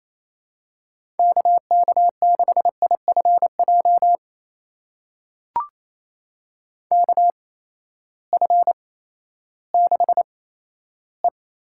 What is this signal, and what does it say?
Morse code, heard as a keyed tone, about 27 wpm, spaced as for 14 wpm, 700 Hz. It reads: KK6IFJ KF6E
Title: USA Call Signs - 6 Characters - 8x Character Spacing - 28wpm